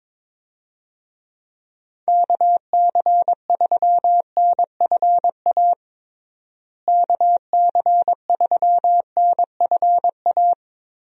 Morse code, heard as a keyed tone, about 22 wpm, 700 Hz. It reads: KC3NFA KC3NFA
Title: USA Call Signs - 6 Characters - Repeated 2x - 22wpm